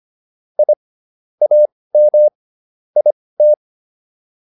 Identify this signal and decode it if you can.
Morse code, heard as a keyed tone, about 25 wpm, spaced as for 18 wpm, 600 Hz. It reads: I AM IT
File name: PR103 18